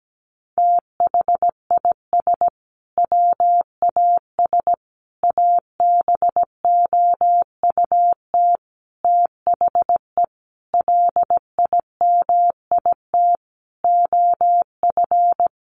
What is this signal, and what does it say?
Morse code, heard as a keyed tone, about 17 wpm, 700 Hz. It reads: THIS WAS ABOUT THE LIMIT OF
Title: Sentences from Top 1500 Words 17wpm